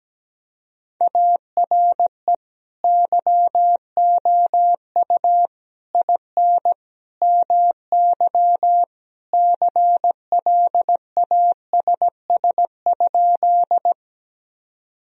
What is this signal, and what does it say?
Morse code, heard as a keyed tone, about 17 wpm, 700 Hz. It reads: ARE YOU IN MY CLASS?
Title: Sentences from Top 1000 Words - Review Part 2 - 17wpm